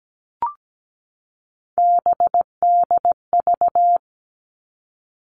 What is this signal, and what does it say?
Morse code, heard as a keyed tone, about 17 wpm, 700 Hz. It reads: BDV
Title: All Cut Numbers - 3-Digit 17wpm